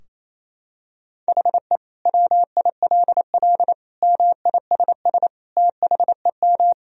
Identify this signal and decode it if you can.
Morse code, heard as a keyed tone, about 28 wpm, 700 Hz. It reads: HE WILL MISS THEM
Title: Sentences from Top 1000 Words - Review Part 3 - 28wpm